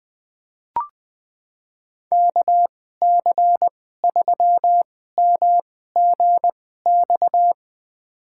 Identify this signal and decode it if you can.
Morse code, heard as a keyed tone, about 20 wpm, spaced as for 14 wpm, 700 Hz. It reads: KC3MGX
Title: USA Call Signs - 6 Characters - 2x Character Spacing - 20wpm